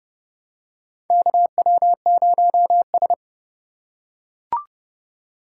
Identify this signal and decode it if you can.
Morse code, heard as a keyed tone, about 31 wpm, 700 Hz. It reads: KW0S
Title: USA Call Signs - 4 Characters - 8x Character Spacing - 30wpm